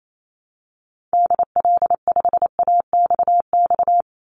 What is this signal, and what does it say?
Morse code, heard as a keyed tone, about 28 wpm, 700 Hz. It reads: DL5AXX
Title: Call Signs - 28wpm